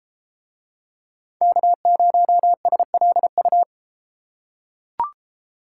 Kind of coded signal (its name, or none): Morse code